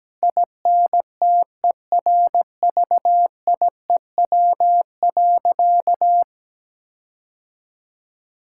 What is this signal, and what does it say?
Morse code, heard as a keyed tone, about 17 wpm, 700 Hz. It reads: INTERVIEW.
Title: Sentences from Top 1600 Words 17wpm